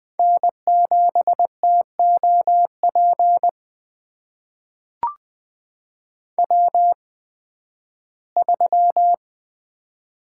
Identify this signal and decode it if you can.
Morse code, heard as a keyed tone, about 20 wpm, spaced as for 13 wpm, 700 Hz. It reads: N7TOP W3
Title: USA Call Signs - 5 Characters - 8x Character Spacing - 20wpm